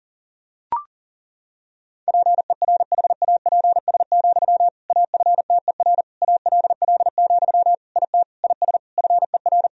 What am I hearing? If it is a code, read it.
Morse code, heard as a keyed tone, about 40 wpm, 700 Hz. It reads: PERHAPS, AFTER ALL, IT IS FER
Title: Sentences from Top 500 Words and Abbreviations 40wpm